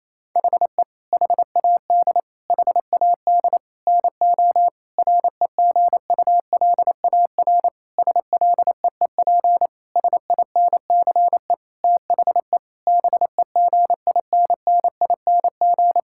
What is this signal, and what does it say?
Morse code, heard as a keyed tone, about 28 wpm, 700 Hz. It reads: HE HAD HAD NO REGULAR SLEEP SINCE THE BEGINNING
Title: Sentences from Top 1000 Words - Review Part 6 - 28wpm